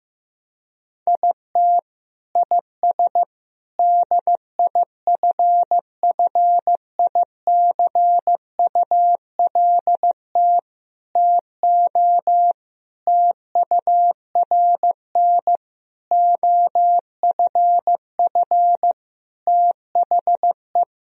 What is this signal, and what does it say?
Morse code, heard as a keyed tone, about 15 wpm, 700 Hz. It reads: IT IS DIFFICULT TO TURN OFF THE
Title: Sentences from Top 1000 Words - Review Part 3 - 15wpm